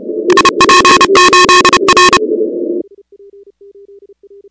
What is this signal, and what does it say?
Morse code, heard as a keyed tone, about 30 wpm, 1080 Hz, with noise. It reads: IP8R
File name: contests / 2019-cw-wpx-kt5j-k5pi- -2330 0-16